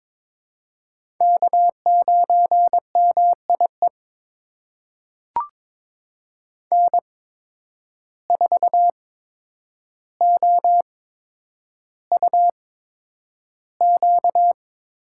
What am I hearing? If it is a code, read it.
Morse code, heard as a keyed tone, about 23 wpm, spaced as for 9 wpm, 700 Hz. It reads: K9MIE N4OUQ
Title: USA Call Signs - 5 Characters - 8x Character Spacing - 22wpm